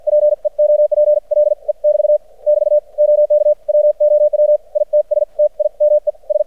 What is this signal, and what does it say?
Morse code, heard as a keyed tone, about 56 wpm, spaced as for 45 wpm, 600 Hz. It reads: PEOPLE <BT> <BT> OK WOW ITS TIME S
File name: archived / CRSnet